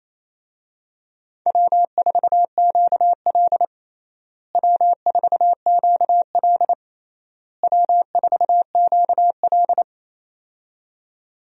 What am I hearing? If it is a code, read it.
Morse code, heard as a keyed tone, about 28 wpm, 700 Hz. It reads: W4QL W4QL W4QL
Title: USA Call Signs - 4 Characters - Repeated 3x - 28wpm